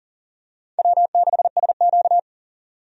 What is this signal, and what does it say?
Morse code, heard as a keyed tone, about 38 wpm, 700 Hz. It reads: W6SQ
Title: USA Call Signs - 4 Characters - 40wpm